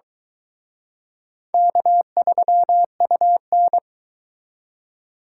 Morse code, heard as a keyed tone, about 23 wpm, 700 Hz. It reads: K3UN